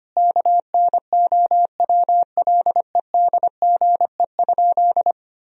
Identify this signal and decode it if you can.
Morse code, heard as a keyed tone, about 25 wpm, 700 Hz. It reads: KNOWLEDGE?
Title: Sentences from Top 1000 Words - Review Part 3 - 25wpm